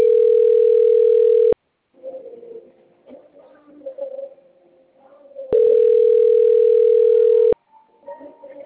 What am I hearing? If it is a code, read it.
Ringback tone (North America), 440 + 480 Hz.
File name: rg-1600-unknown- - - .49137